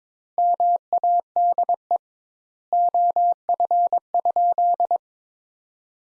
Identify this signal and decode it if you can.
Morse code, heard as a keyed tone, about 21 wpm, 700 Hz. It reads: MADE OF?
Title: Sentences Top 200 Words - 22wpm